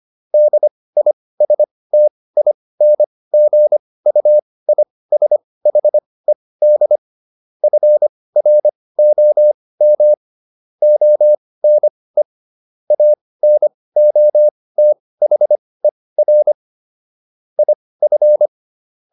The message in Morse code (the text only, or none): DISTINGUISHED FROM ONE ANOTHER IF